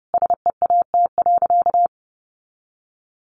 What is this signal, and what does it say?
Morse code, heard as a keyed tone, about 30 wpm, 700 Hz. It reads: SEAT.
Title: Sentences from Top 1500 Words 30wpm